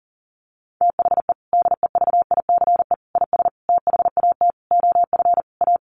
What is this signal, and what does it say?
Morse code, heard as a keyed tone, about 37 wpm, 700 Hz. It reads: THE DEVICE IS THAT OF A
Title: Sentences from Top 1500 Words 40wpm